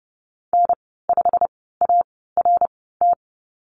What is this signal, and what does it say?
Morse code, heard as a keyed tone, about 30 wpm, spaced as for 17 wpm, 700 Hz. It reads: N5ART
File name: Lesson 19g - Letter w - Call Signs 30wpm